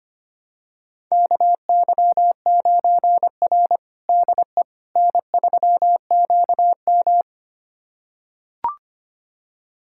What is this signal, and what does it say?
Morse code, heard as a keyed tone, about 25 wpm, 700 Hz. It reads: KY9R DE N3QM
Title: USA Callsign DE Callsign - 4 Characters - 25wpm